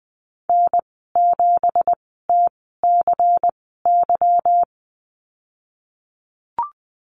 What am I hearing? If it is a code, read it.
Morse code, heard as a keyed tone, about 20 wpm, spaced as for 15 wpm, 700 Hz. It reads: N7TCY
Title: 167 Lesson 27g - Number 7 - Call Signs 20wpm